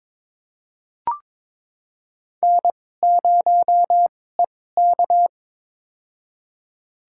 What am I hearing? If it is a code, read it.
Morse code, heard as a keyed tone, about 22 wpm, spaced as for 16 wpm, 700 Hz. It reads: N0EK
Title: USA Call Signs - 2x Character Spacing - 22wpm